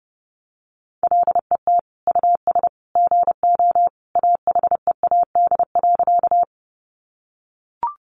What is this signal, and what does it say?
Morse code, heard as a keyed tone, about 30 wpm, 700 Hz. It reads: LET US GO AHEAD.
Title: Sentences from Top 1100 Words 30wpm